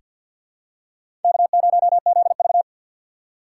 Morse code, heard as a keyed tone, about 49 wpm, 700 Hz. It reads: K0ZV
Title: USA Call Signs - 4 Characters - 6x Character Spacing - 50wpm